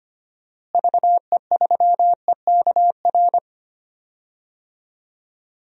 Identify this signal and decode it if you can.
Morse code, heard as a keyed tone, about 25 wpm, 700 Hz. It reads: VE3EKR
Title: DXCC-100 Call Signs - 6 Characters - Repeated 3x - 25wpm